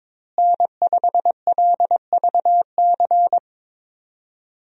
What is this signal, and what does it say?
Morse code, heard as a keyed tone, about 22 wpm, 700 Hz. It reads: N5LVC
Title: USA Call Signs - 5 Characters - Repeated 2x - 22wpm